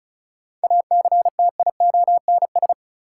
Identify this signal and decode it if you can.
Morse code, heard as a keyed tone, about 34 wpm, 700 Hz. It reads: ACTIONS
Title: Sentences from Top 1000 Words - Review Part 6 - New Word Drill - 35wpm